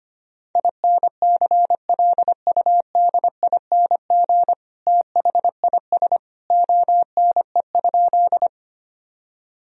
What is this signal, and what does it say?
Morse code, heard as a keyed tone, about 25 wpm, 700 Hz. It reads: INCLUDING THIS ONE?
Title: Sentences from Top 1000 Words - Review Part 4 - 25wpm